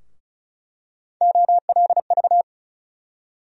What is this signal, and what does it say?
Morse code, heard as a keyed tone, about 34 wpm, 700 Hz. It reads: OLV
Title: USA Call Sign Postfixes - 3 Letters 35wpm